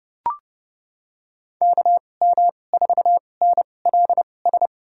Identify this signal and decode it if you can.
Morse code, heard as a keyed tone, about 30 wpm, spaced as for 22 wpm, 700 Hz. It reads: KM4NLS
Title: USA Call Signs - 6 Characters - 2x Character Spacing - 30wpm